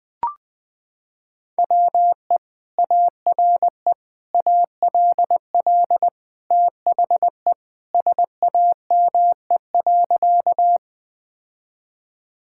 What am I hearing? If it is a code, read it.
Morse code, heard as a keyed tone, about 21 wpm, 700 Hz. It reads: WE ARE ALL THE SAME.
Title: Sentences with 2-Word Phrases - 20wpm